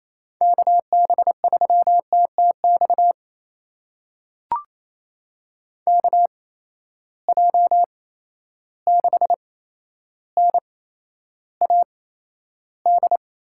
Morse code, heard as a keyed tone, about 28 wpm, spaced as for 12 wpm, 700 Hz. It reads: KB3TTX KJ6NAD
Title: USA Call Signs - 6 Characters - 8x Character Spacing - 28wpm